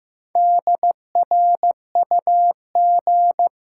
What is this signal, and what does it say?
Morse code, heard as a keyed tone, about 15 wpm, 700 Hz. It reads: DRUG
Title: Sets of 5 Words - New Word Drill - 15wpm